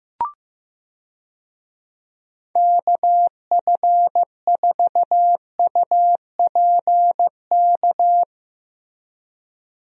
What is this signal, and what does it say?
Morse code, heard as a keyed tone, about 15 wpm, 700 Hz. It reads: KF4UPK
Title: USA Call Signs - 6 Characters - Repeated 3x - 15wpm